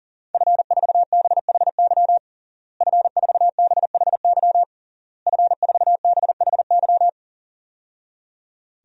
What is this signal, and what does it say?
Morse code, heard as a keyed tone, about 40 wpm, 700 Hz. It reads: F4BHY F4BHY F4BHY
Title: DXCC-100 Call Signs - 5 Characters - Repeated 3x - 40wpm